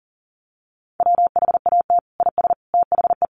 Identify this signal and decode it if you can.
Morse code, heard as a keyed tone, about 37 wpm, 700 Hz. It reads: WHAT IS THE
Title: Sentences from Top 1300 Words 40wpm